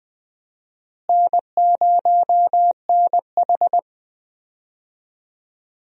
Morse code, heard as a keyed tone, about 20 wpm, 700 Hz. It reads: N0NH
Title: USA Call Signs - 4 Characters - Repeated 2x - 20wpm